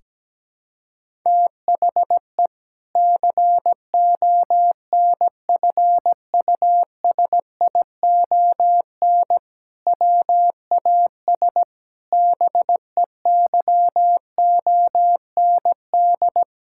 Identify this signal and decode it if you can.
Morse code, heard as a keyed tone, about 17 wpm, 700 Hz. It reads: THE CONFUSION WAS BEYOND
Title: Sentences from Top 1600 Words 17wpm